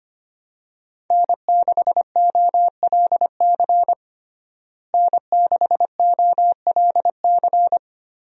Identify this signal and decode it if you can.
Morse code, heard as a keyed tone, about 25 wpm, 700 Hz. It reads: N6OLC N6OLC
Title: USA Call Signs - 5 Characters - Repeated 2x - 25wpm